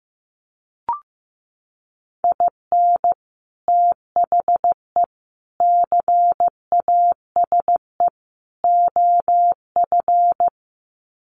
Morse code, heard as a keyed tone, about 15 wpm, 700 Hz. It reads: IN THE CASE OF
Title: Sets of 4 Words 15wpm